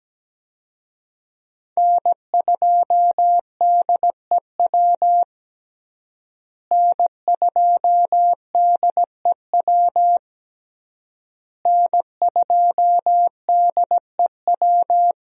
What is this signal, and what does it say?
Morse code, heard as a keyed tone, about 17 wpm, 700 Hz. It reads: N2DEW N2DEW N2DEW
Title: USA Call Signs - 5 Characters - Repeated 3x - 17wpm